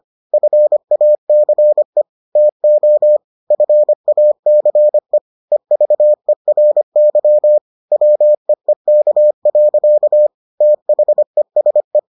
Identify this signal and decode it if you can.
Morse code, heard as a keyed tone, about 25 wpm, 600 Hz. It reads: FACE TO FACE EVERY WEEK. THESE